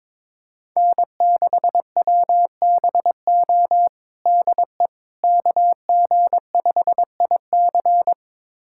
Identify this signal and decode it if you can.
Morse code, heard as a keyed tone, about 22 wpm, 700 Hz. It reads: N6WBO DE KG5IC
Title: USA Callsign DE Callsign - 5 Characters - 22wpm